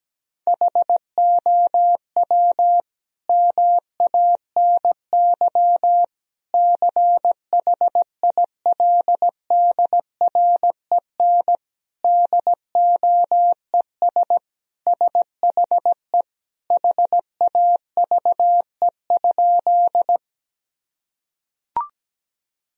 Morse code, heard as a keyed tone, about 17 wpm, 700 Hz. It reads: HOW MANY CHILDREN DOES SHE HAVE?
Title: Sentences from Top 1000 Words - Review Part 1 - 17wpm